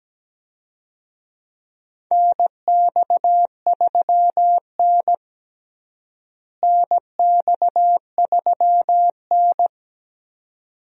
Morse code, heard as a keyed tone, about 17 wpm, 700 Hz. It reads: NX3N NX3N
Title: USA Call Signs - 4 Characters - Repeated 2x - 17wpm